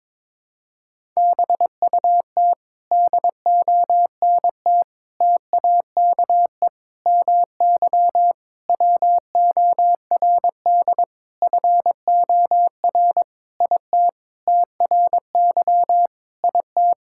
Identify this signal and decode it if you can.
Morse code, heard as a keyed tone, about 22 wpm, 700 Hz. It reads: BUT DONT TAKE MY WORD FOR IT TRY IT
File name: Sentences with 7-Word Phrases - 22wpm